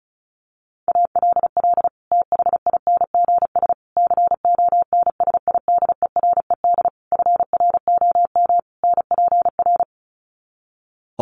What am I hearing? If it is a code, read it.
Morse code, heard as a keyed tone, about 34 wpm, 700 Hz. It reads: ALL THINGS CONSIDERED FROM NPR
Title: Sets of 5 Words Encore 35wpm